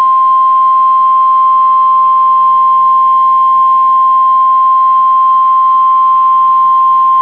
Emergency Alert System (SAME) tones (the 1050 Hz attention tone). It is heard from 0.6 to 6.5 s.